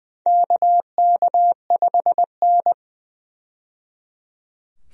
Morse code, heard as a keyed tone, about 20 wpm, 700 Hz. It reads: KK5N
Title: USA Call Signs - 4 Characters - 20wpm